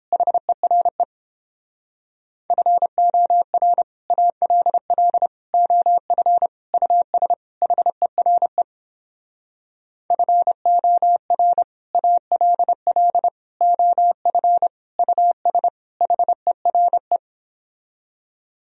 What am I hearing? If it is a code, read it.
Morse code, heard as a keyed tone, about 29 wpm, 700 Hz. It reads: HERE FOR ALL OF US HERE FOR ALL OF US HERE